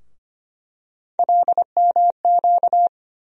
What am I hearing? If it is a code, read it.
Morse code, heard as a keyed tone, about 25 wpm, 700 Hz. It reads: LMQ
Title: USA Call Sign Postfixes - 3 Letters 25wpm